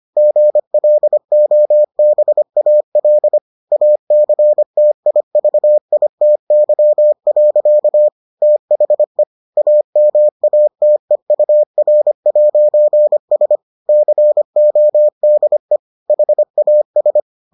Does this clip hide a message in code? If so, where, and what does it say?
Yes: Morse code from 0.2 to 17.2 s, heard as a keyed tone, about 25 wpm, 600 Hz. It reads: GLOBAL ACTIVITY. THE AMATEUR'S CODE HAS